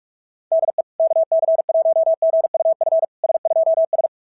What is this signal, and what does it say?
Morse code, heard as a keyed tone, about 44 wpm, 660 Hz. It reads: DE KC1GUF S2S
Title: SOTA QSO Practice 45wpm